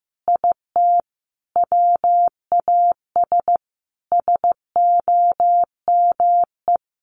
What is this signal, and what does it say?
Morse code, heard as a keyed tone, about 15 wpm, 700 Hz. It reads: IT WAS SOME
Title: Sentences from Top 1500 Words 15wpm